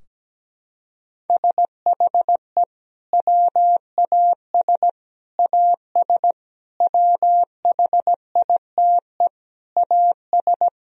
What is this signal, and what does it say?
Morse code, heard as a keyed tone, about 17 wpm, 700 Hz. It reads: SHE WAS AS WHITE AS
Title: Sentences from Top 1000 Words - Review Part 3 - 17wpm